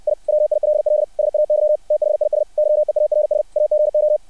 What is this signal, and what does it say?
Morse code, heard as a keyed tone, about 60 wpm, 600 Hz. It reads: I WILL SAY THIS YEARS AGO
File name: archived / CRSnet